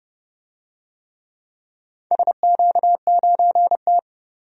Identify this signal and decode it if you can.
Morse code, heard as a keyed tone, about 31 wpm, 700 Hz. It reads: SQ9T
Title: DXCC-100 Call Signs - 4 Characters - 30wpm